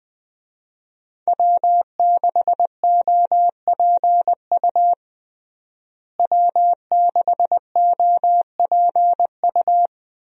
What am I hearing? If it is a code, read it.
Morse code, heard as a keyed tone, about 20 wpm, 700 Hz. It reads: W6OPU W6OPU
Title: USA Call Signs - 5 Characters - Repeated 2x - 20wpm